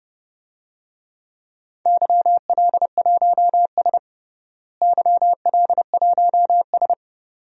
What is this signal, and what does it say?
Morse code, heard as a keyed tone, about 30 wpm, 700 Hz. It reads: YL1S YL1S
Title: DXCC-100 Call Signs - 4 Characters - Repeated 2x - 30wpm